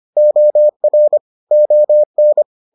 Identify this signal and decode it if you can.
Morse code, heard as a keyed tone, about 25 wpm, 600 Hz. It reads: OR ON